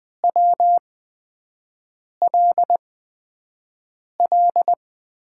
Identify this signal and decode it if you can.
Morse code, heard as a keyed tone, about 20 wpm, 700 Hz. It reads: W L L
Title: USA Call Signs - 5 Characters - 8x Character Spacing - 20wpm